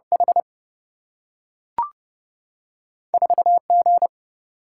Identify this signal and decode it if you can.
Morse code, heard as a keyed tone, about 30 wpm, 700 Hz. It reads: H 4 G
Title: Single-Letter Single-Number 30wpm